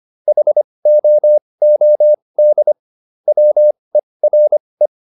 Morse code, heard as a keyed tone, about 25 wpm, spaced as for 20 wpm, 600 Hz. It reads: HOOD WERE